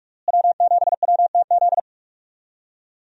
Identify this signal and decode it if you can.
Morse code, heard as a keyed tone, about 48 wpm, 700 Hz. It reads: W7WTZ
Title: USA Call Signs - 5 Characters - Repeated 3x - 45wpm